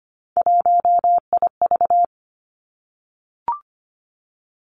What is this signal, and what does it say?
Morse code, heard as a keyed tone, about 25 wpm, 700 Hz. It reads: 1IV
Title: Three Challenging Letter-Number Combinations 25wpm